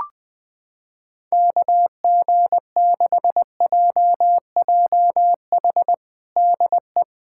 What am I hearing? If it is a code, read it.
Morse code, heard as a keyed tone, about 20 wpm, 700 Hz. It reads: KG6JJH DE
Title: USA Callsign DE Callsign 20wpm